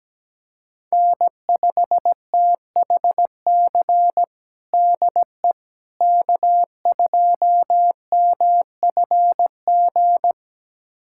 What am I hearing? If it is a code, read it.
Morse code, heard as a keyed tone, about 17 wpm, 700 Hz. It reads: N5THC DE K2MFG
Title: USA Callsign DE Callsign - 5 Characters - 17wpm